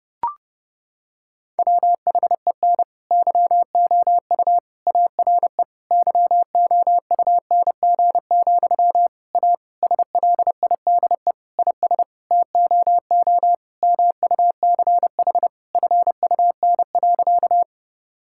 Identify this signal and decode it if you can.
Morse code, heard as a keyed tone, about 32 wpm, 700 Hz. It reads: WHEN YOU ARE YOUNG, A SLIDE IS TOO MUCH FUN.
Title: Sentences from Top 1600 Words 30wpm